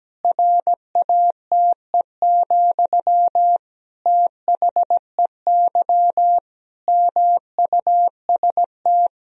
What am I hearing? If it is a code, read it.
Morse code, heard as a keyed tone, about 17 wpm, 700 Hz. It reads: RATE, THEY MUST
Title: Sentences from Top 1000 Words - Review Part 4 - 17wpm